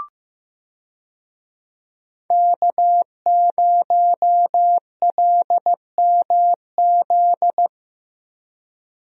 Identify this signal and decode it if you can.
Morse code, heard as a keyed tone, about 15 wpm, 700 Hz. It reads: K0LMZ
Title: USA Call Signs - 5 Characters - Repeated 3x - 15wpm